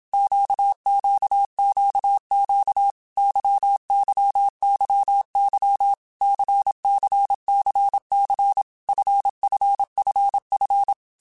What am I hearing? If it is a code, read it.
Morse code, heard as a keyed tone, about 27 wpm, 800 Hz. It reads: QQQQ YYYY CCCC FFFF
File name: Basic Letter Groups By7Groups / LFCYQ / LFCYQ@25